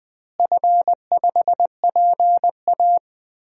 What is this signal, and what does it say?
Morse code, heard as a keyed tone, about 20 wpm, 700 Hz. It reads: F5PA
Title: DXCC-100 Call Signs - 4 Characters - Repeated 2x - 20wpm